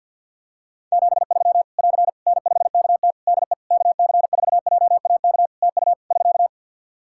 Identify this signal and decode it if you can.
Morse code, heard as a keyed tone, about 47 wpm, 700 Hz. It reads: 73 <AR> N5KT DE KC4JAK TU <SK>
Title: K1USN SST QSO Practice 50wpm